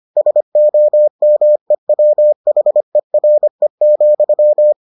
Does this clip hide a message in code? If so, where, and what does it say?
Yes: Morse code from 0.2 to 4.7 s, heard as a keyed tone, about 25 wpm, 600 Hz. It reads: SOMEWHERE,